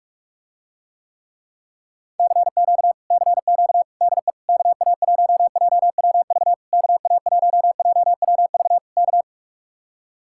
Morse code, heard as a keyed tone, about 49 wpm, 700 Hz. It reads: CQ CQ DE KA1JWV KA1JWV K